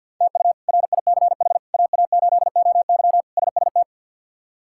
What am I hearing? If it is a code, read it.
Morse code, heard as a keyed tone, about 52 wpm, 700 Hz. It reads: TU RICH AA8OY SST
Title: K1USN SST QSO Practice - 50wpm